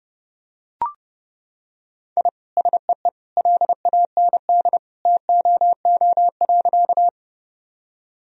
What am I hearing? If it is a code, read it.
Morse code, heard as a keyed tone, about 30 wpm, 700 Hz. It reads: I SEE LAND TOO.
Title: Sentences from Top 1000 Words - Review Part 1 - 30wpm